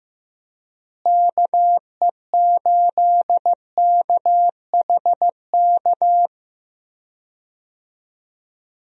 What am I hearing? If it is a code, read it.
Morse code, heard as a keyed tone, about 15 wpm, 700 Hz. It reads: KE8KHK
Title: USA Call Signs - 6 Characters - Repeated 3x - 15wpm